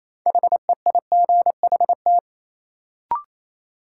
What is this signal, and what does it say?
Morse code, heard as a keyed tone, about 28 wpm, 700 Hz. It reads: HEIGHT
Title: Sentences from Top 1600 Words - New Word Drill Alt - 28wpm